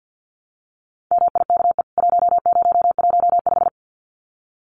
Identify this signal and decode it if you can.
Morse code, heard as a keyed tone, about 50 wpm, 700 Hz. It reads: MIKE 1015
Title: CWT Exchange 50wpm